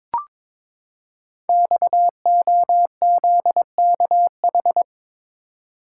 Morse code, heard as a keyed tone, about 22 wpm, 700 Hz. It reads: XOZKH
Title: Code Groups - 5 Letter:Numbers - 22wpm